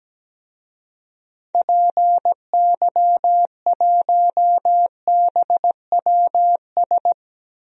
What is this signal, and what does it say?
Morse code, heard as a keyed tone, about 17 wpm, 700 Hz. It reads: PY1BWS
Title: DXCC-100 Call Signs - 6 Characters - 17wpm